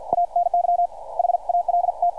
Morse code, heard as a keyed tone, about 50 wpm, spaced as for 44 wpm, 700 Hz, with radio noise. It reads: ANY HALT
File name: archived / CRSnet